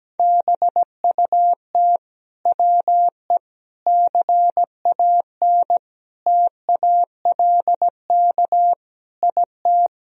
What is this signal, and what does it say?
Morse code, heard as a keyed tone, about 17 wpm, 700 Hz. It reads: BUT WE CAN TALK IT
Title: Sentences from Top 1000 Words - Review Part 4 - 17wpm